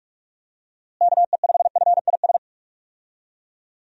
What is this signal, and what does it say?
Morse code, heard as a keyed tone, about 46 wpm, 700 Hz. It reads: KE5FIS
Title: USA Call Signs - 6 Characters - Repeated 2x - 45wpm